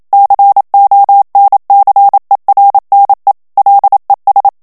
Morse code, heard as a keyed tone, about 27 wpm, 800 Hz. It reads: CONCERNE LES
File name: CW / FICHIERS-MP3 / FRANCAIS TEXTE CLAIR 1 VITESSE 26 MOTS MINUTES